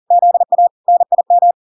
Morse code, heard as a keyed tone, about 40 wpm, 700 Hz. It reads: ZA NIM